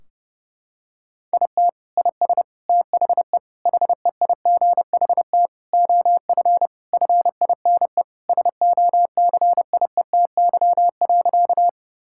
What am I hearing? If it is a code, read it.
Morse code, heard as a keyed tone, about 30 wpm, 700 Hz. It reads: IT IS THE HEIGHT OF FINE SOCIETY.
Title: Sentences from Top 1600 Words 30wpm